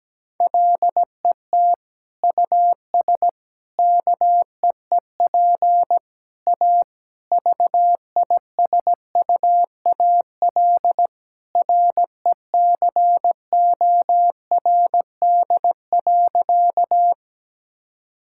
Morse code, heard as a keyed tone, about 17 wpm, 700 Hz. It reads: LET US KEEP A VISUAL RECORD.
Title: Sentences from Top 1000 Words - Review Part 6 - 17wpm